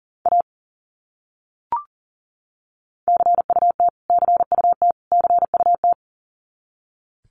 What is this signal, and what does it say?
Morse code, heard as a keyed tone, about 40 wpm, 700 Hz. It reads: A CUT CUT CUT
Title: Top 300 Words Repeated 40wpm